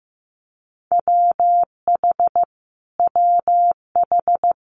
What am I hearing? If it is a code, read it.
Morse code, heard as a keyed tone, about 15 wpm, 700 Hz. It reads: WH WH
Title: Multi-letter Phonemes Repeated 3x 15wpm